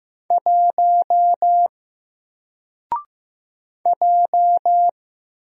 Morse code, heard as a keyed tone, about 15 wpm, 700 Hz. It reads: 1 J
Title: Single Character (Letter:Number:Punctuation:Prosign) Alt 15wpm